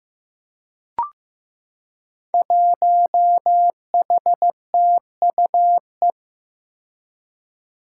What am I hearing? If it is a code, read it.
Morse code, heard as a keyed tone, about 15 wpm, 700 Hz. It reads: 1HTUE
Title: Five Challenging Letter-Number Combinations 15wpm